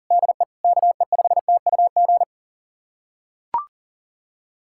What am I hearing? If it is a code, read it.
Morse code, heard as a keyed tone, about 40 wpm, 700 Hz. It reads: DE KE5TUG
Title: USA Callsign DE Callsign - 6 Characters - 40wpm